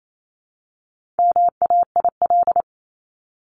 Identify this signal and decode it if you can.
Morse code, heard as a keyed tone, about 28 wpm, 700 Hz. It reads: MAIL
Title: Top 1751-2000 Words 28wpm